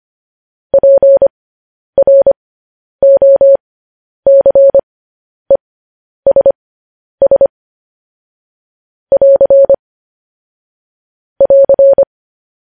Morse code, heard as a keyed tone, about 25 wpm, spaced as for 10 wpm, 550 Hz. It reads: PROCESS <AR> <AR>
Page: news-headlines-in-morse-code-at-10-wpm